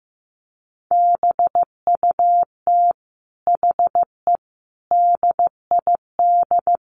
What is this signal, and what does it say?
Morse code, heard as a keyed tone, about 15 wpm, 700 Hz. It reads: BUT HE DID
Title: Sentences from Top 1300 Words 15wpm